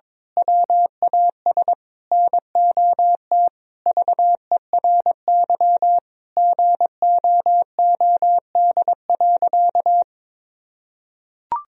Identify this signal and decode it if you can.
Morse code, heard as a keyed tone, about 22 wpm, 700 Hz. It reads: WAS NOT VERY GOOD.
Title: Sentences from Top 1000 Words - Review Part 3 - 22wpm